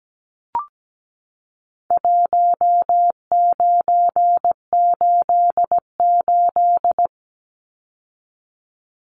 Morse code, heard as a keyed tone, about 17 wpm, 700 Hz. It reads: 1988
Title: Numbers - Contest Style - 4-Digit 17wpm